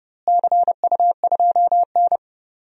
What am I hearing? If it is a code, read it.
Morse code, heard as a keyed tone, about 30 wpm, 700 Hz. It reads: CU2N
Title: DXCC-100 Call Signs - 4 Characters - Repeated 2x - 30wpm